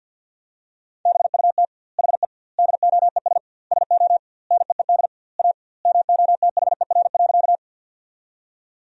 Morse code, heard as a keyed tone, about 50 wpm, 700 Hz. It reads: BUT HE DOES SO NEED A MOTHER.
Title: 2x Word-Spacing - Sentences from Top 200 Words - 50wpm